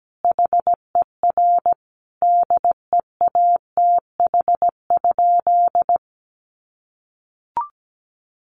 Morse code, heard as a keyed tone, about 17 wpm, 700 Hz. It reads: HER DEATH?
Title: Sentences from Top 1500 Words 17wpm